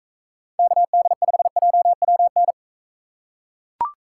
Morse code, heard as a keyed tone, about 38 wpm, 700 Hz. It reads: KD5JWN